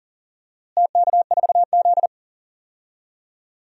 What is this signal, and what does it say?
Morse code, heard as a keyed tone, about 40 wpm, 700 Hz. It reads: TK4Z
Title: DXCC-100 Call Signs - 4 Characters - Repeated 2x - 40wpm